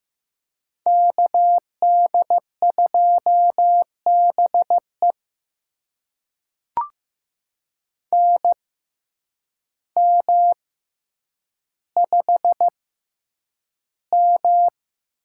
Morse code, heard as a keyed tone, about 15 wpm, spaced as for 7 wpm, 700 Hz. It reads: KD2BEN M5M
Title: USA Call Signs - 6x Character Spacing - 15wpm